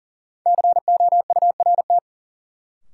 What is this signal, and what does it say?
Morse code, heard as a keyed tone, about 38 wpm, 700 Hz. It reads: COURT
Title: Sentences from Top 1600 Words - New Word Drill - 40wpm